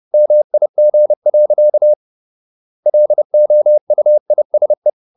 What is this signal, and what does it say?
Morse code, heard as a keyed tone, about 30 wpm, 600 Hz. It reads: MIG. LOUISE
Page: Ferie 30 wpm